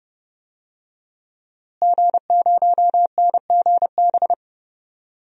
Morse code, heard as a keyed tone, about 30 wpm, 700 Hz. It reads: G0NGB
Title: DXCC-100 Call Signs - 5 Characters - Repeated 3x - 30wpm